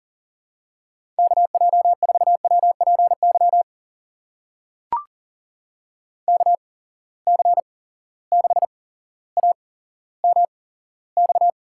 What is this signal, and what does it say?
Morse code, heard as a keyed tone, about 42 wpm, spaced as for 17 wpm, 700 Hz. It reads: KJ4WPY KC6AMX
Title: USA Call Signs - 8x Character Spacing - 40wpm